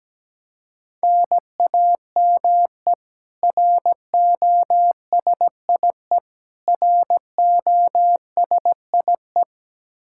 Morse code, heard as a keyed tone, about 17 wpm, 700 Hz. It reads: NAME ROSIE ROSIE
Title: K1USN SST QSO Practice 17wpm